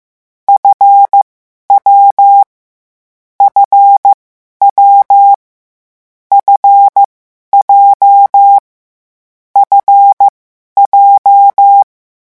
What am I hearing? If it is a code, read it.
Morse code, heard as a keyed tone, about 15 wpm, spaced as for 11 wpm, 800 Hz. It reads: F W FW F J F J
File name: morse mp3 / lecon7-vitesse 14